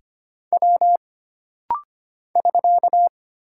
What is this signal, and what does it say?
Morse code, heard as a keyed tone, about 25 wpm, 700 Hz. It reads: W <SK>
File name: Single Character (Letter:Number:Punctuation:Prosign) - Rapid-Fire - 25wpm